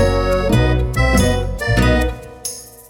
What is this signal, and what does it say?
There is no code.